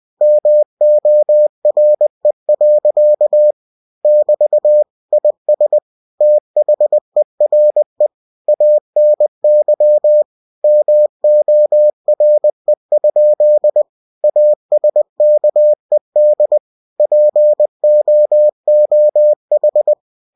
Morse code, heard as a keyed tone, about 20 wpm, 600 Hz. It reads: MORE. <BT> IS THERE ANY MORE? ASKED POOH
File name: cw-books / winnie-the-pooh / winnie-20wpm-page-0016